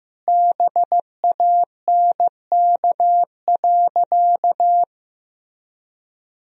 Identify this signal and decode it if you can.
Morse code, heard as a keyed tone, about 15 wpm, 700 Hz. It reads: BANK.
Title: Sentences from Top 1000 Words - Review Part 4 - 15wpm